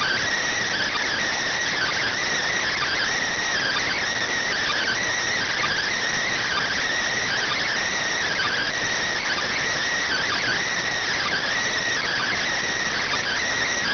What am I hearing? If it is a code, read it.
Emergency Alert System (SAME) tones (header bursts), heard as harsh buzzing bursts. It is heard from 0.4 to 13.4 s.